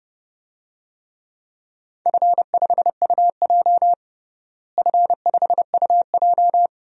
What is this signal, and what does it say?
Morse code, heard as a keyed tone, about 29 wpm, 700 Hz. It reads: F5UJ F5UJ
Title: DXCC-100 Call Signs - 4 Characters - Repeated 2x - 30wpm